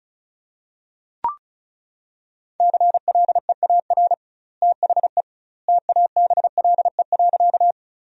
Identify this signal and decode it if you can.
Morse code, heard as a keyed tone, about 34 wpm, 700 Hz. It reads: CLEAR THE TABLE.
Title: 2x Word-Spacing - Sentences from Top 500 Words - 35wpm